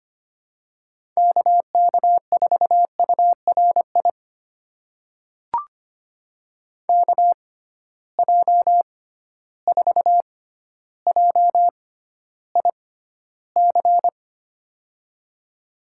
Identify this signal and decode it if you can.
Morse code, heard as a keyed tone, about 25 wpm, spaced as for 13 wpm, 700 Hz. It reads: KK4URI KJ4JIC
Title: USA Call Signs - 6x Character Spacing - 25wpm